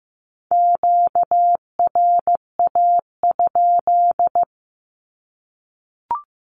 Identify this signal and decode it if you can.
Morse code, heard as a keyed tone, about 15 wpm, 700 Hz. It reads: QRA?
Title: Q-Codes 15wpm